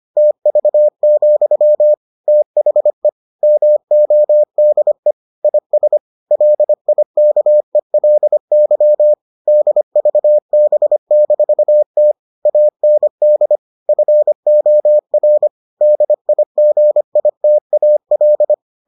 Morse code, heard as a keyed tone, about 25 wpm, 600 Hz. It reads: TV, THE MODE IS LIKELY DVB-T AND FOR DIGITAL